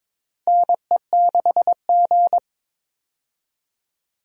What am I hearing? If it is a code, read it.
Morse code, heard as a keyed tone, about 22 wpm, 700 Hz. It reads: NE6G
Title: USA Call Signs - 4 Characters - 22wpm